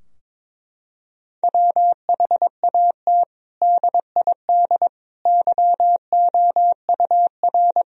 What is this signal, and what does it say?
Morse code, heard as a keyed tone, about 22 wpm, 700 Hz. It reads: WHAT DID YOUR
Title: Sentences from Top 1000 Words - Review Part 4 - 22wpm